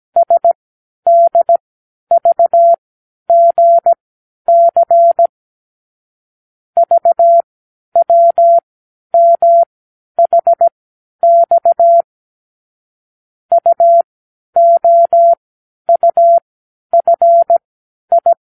Morse code, heard as a keyed tone, about 17 wpm, spaced as for 10 wpm, 680 Hz. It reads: SDVGC VWMHX UOUFI
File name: k7qo / Lesson 36 (A-X)